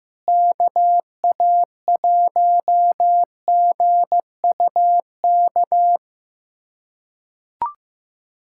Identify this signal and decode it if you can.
Morse code, heard as a keyed tone, about 16 wpm, 700 Hz. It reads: KA1GUK
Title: USA Callsign DE Callsign - 6 Characters - 15wpm